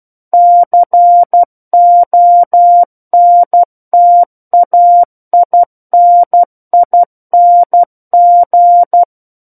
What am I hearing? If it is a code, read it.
Morse code, heard as a keyed tone, about 12 wpm, 700 Hz. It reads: CONTAINING
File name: Book One / 03-Chapter 3 - 12 WPM